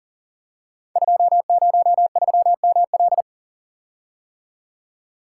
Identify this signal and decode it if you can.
Morse code, heard as a keyed tone, about 42 wpm, 690 Hz. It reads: 203ML
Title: Code Groups - 5 Letter:Numbers - 40wpm